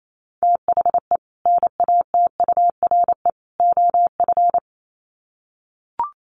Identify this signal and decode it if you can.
Morse code, heard as a keyed tone, about 28 wpm, 700 Hz. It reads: THE NATURE OF
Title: Sets of 3 Words 28wpm